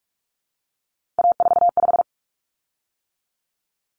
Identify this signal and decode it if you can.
Morse code, heard as a keyed tone, about 44 wpm, 700 Hz. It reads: A45
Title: Common Cut Numbers - 3-Digit 45wpm